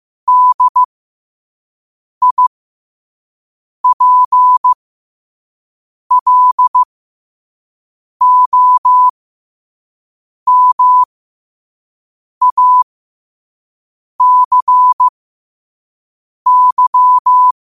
Morse code, heard as a keyed tone, about 15 wpm, spaced as for 5 wpm, 1000 Hz. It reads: DIPLOMACY
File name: qotdfiles / 5wpm